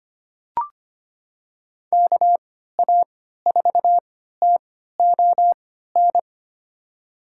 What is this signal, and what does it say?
Morse code, heard as a keyed tone, about 25 wpm, spaced as for 14 wpm, 700 Hz. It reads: KA4TON
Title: USA Call Signs - 6 Characters - 3x Character Spacing - 25wpm